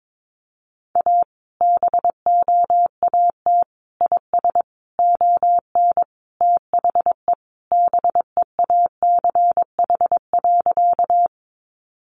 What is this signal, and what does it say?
Morse code, heard as a keyed tone, about 22 wpm, 700 Hz. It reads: A BOAT IS ON THE BEACH.
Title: Sentences from Top 1500 Words 22wpm